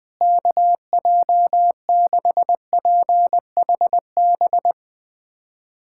Morse code, heard as a keyed tone, about 20 wpm, 700 Hz. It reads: KJ6PHB
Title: USA Callsign DE Callsign - 6 Characters - 20wpm